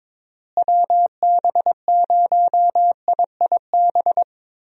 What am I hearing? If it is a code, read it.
Morse code, heard as a keyed tone, about 22 wpm, 700 Hz. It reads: WB0IIB
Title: USA Call Signs - 6 Characters - Repeated 3x - 22wpm